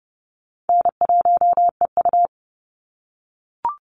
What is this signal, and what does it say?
Morse code, heard as a keyed tone, about 30 wpm, 700 Hz. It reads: N1EU
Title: NA SOTA Activator Callsigns 30wpm